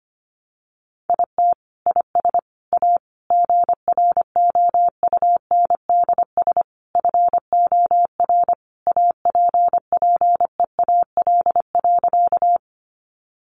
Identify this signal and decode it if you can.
Morse code, heard as a keyed tone, about 25 wpm, 700 Hz. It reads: IT IS A GROUNDS FOR APPEAL.
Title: Sentences from Top 1300 Words 25wpm